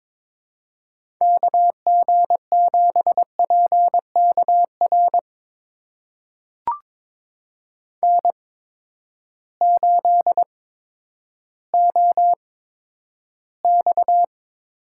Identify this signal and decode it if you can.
Morse code, heard as a keyed tone, about 22 wpm, spaced as for 11 wpm, 700 Hz. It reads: KG7PKR N8OX
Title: USA Call Signs - 8x Character Spacing - 22wpm